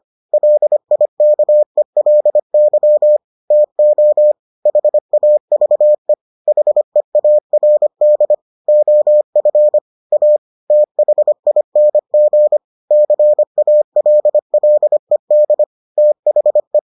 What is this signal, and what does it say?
Morse code, heard as a keyed tone, about 25 wpm, 600 Hz. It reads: LIKELY TO HAVE HEARD OF A THING CALLED THE